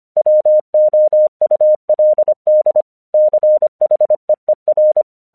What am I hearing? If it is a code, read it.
Morse code, heard as a keyed tone, about 25 wpm, 610 Hz. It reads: WOULD CHEER
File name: 1 English 25wpm / AQC14